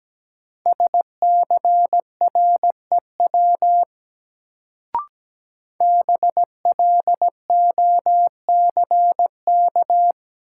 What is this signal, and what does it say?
Morse code, heard as a keyed tone, about 17 wpm, 700 Hz. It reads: SCREW BLOCK
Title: Sentences from Top 1600 Words - New Word Drill Alt - 17wpm